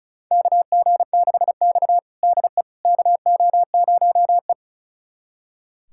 Morse code, heard as a keyed tone, about 36 wpm, 700 Hz. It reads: KG6X DE KO0E
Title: USA Callsign DE Callsign - 4 Characters - 35wpm